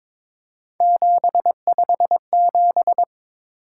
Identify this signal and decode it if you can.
Morse code, heard as a keyed tone, about 22 wpm, 700 Hz. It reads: 757
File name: practice / 22wpm / Three Letter-Number Combinations - 22wpm